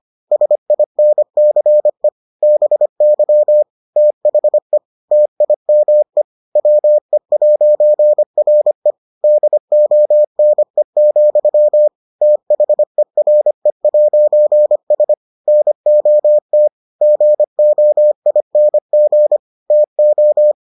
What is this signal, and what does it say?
Morse code, heard as a keyed tone, about 25 wpm, 600 Hz. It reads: SINCE BY THE TIME WE'RE DONE, THERE'S NOT GOING TO